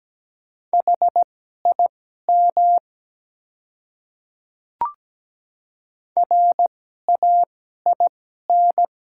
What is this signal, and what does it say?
Morse code, heard as a keyed tone, about 17 wpm, spaced as for 12 wpm, 700 Hz. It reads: HIM RAIN
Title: 2x Character Spacing - Top 300 Words - 17wpm